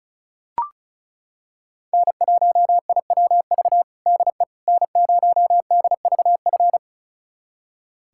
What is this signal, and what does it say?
Morse code, heard as a keyed tone, about 36 wpm, 700 Hz. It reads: N1IWV DE N0DVF
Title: USA Callsign DE Callsign - 5 Characters - 35wpm